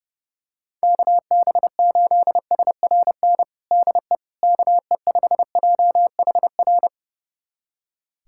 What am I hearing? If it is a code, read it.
Morse code, heard as a keyed tone, about 30 wpm, 700 Hz. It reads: KB8SRN DE KE5JHR
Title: USA Callsign DE Callsign - 6 Characters - 30wpm